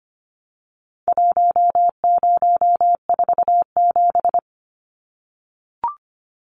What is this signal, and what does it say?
Morse code, heard as a keyed tone, about 25 wpm, 700 Hz. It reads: 1047
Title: Numbers - Contest Style - 4-Digit 25wpm